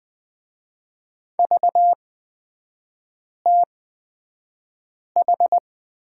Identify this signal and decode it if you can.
Morse code, heard as a keyed tone, about 20 wpm, spaced as for 5 wpm, 700 Hz. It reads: VTH
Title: 243 Lesson 39e - Letter z - Three Characters 20wpm